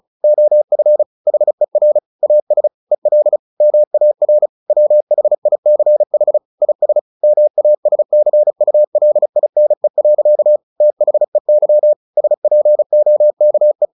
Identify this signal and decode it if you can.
Morse code, heard as a keyed tone, about 34 wpm, 600 Hz. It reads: OF HER AS EL MAR WHICH IS MASCULINE. THEY SPOKE
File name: the old man and the sea / the old man and the sea 1 35wpm